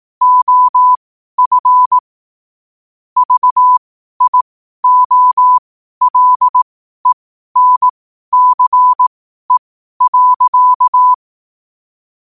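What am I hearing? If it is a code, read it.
Morse code, heard as a keyed tone, about 18 wpm, spaced as for 12 wpm, 1000 Hz. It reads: OF VIOLENCE.
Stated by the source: Morse code quotes at 12 words per minute, by ai4qr.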